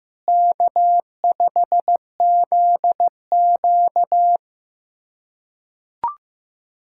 Morse code, heard as a keyed tone, about 16 wpm, 700 Hz. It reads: K5ZQ
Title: K1USN SST QSO Practice - 15wpm